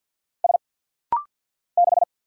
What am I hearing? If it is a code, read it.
Morse code, heard as a keyed tone, about 49 wpm, 700 Hz. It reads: S 6
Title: Single Letter-Number - Rapid-Fire 50wpm